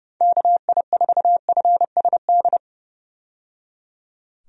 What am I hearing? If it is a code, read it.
Morse code, heard as a keyed tone, about 30 wpm, 700 Hz. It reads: KI4FSD
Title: USA Call Signs - 6 Characters - 30wpm